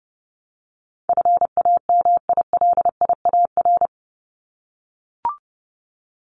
Morse code, heard as a keyed tone, about 30 wpm, 700 Hz. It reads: FAMILIAR
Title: Top 751-1000 Words 30wpm